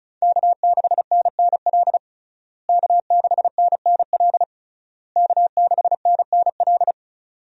Morse code, heard as a keyed tone, about 34 wpm, 700 Hz. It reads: K6NNL K6NNL K6NNL
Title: USA Call Signs - 5 Characters - Repeated 3x - 35wpm